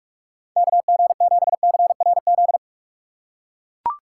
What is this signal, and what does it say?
Morse code, heard as a keyed tone, about 44 wpm, 700 Hz. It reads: KG7CRZ
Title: USA Call Signs - 6x Character Spacing - 45wpm